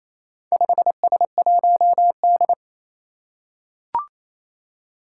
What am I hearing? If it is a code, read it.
Morse code, heard as a keyed tone, about 28 wpm, 700 Hz. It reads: 5S1D
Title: Four Challenging Letter-Number Combinations 28wpm